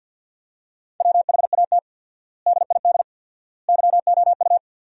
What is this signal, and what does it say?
Morse code, heard as a keyed tone, about 50 wpm, 700 Hz. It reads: WHAT DID YOU
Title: 4x Word-Spacing - Sentences from Top 300 Words - 50wpm